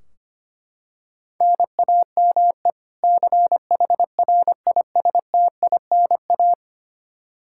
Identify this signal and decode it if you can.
Morse code, heard as a keyed tone, about 25 wpm, 700 Hz. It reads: NAME CHRISTINA
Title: Practice QSO - Exchange 1 - Beginning - 25wpm